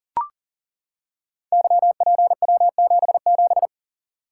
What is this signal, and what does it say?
Morse code, heard as a keyed tone, about 42 wpm, 700 Hz. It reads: YPW77
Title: Code Groups - 5 Letter:Numbers - 40wpm